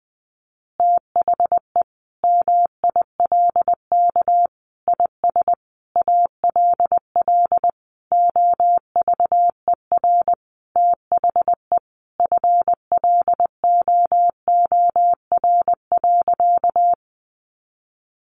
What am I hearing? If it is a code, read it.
Morse code, heard as a keyed tone, about 20 wpm, 700 Hz. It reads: THE MILK IS ALL OVER THE FLOOR.
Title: Additional Easy Phrases 20wpm